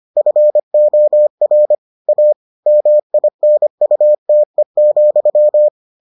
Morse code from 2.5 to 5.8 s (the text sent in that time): MINUTE,